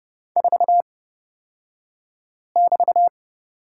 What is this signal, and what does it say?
Morse code, heard as a keyed tone, about 30 wpm, spaced as for 5 wpm, 700 Hz. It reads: 4<BT>
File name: Lesson 36d - BT - Two Characters 30wpm